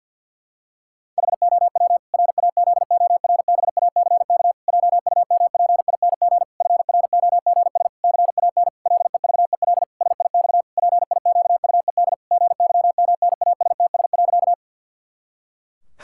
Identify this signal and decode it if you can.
Morse code, heard as a keyed tone, about 50 wpm, 700 Hz. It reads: HOW RAZORBACK JUMPING FROGS CAN LEVEL SIX PIQUED GYMNASTS.
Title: Pangram Challenge v2 - 50wpm